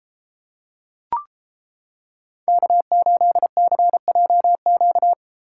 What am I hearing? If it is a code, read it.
Morse code, heard as a keyed tone, about 31 wpm, 700 Hz. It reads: K8CJQ